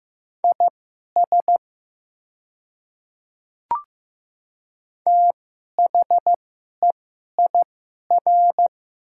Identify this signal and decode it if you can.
Morse code, heard as a keyed tone, about 15 wpm, spaced as for 11 wpm, 700 Hz. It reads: IS THEIR
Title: 2x Character Spacing - Top 100 Words - 15wpm